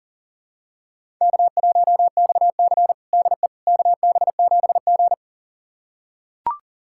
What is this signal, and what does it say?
Morse code, heard as a keyed tone, about 40 wpm, 700 Hz. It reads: K1XC DE KB7G
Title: USA Callsign DE Callsign - 4 Characters - 40wpm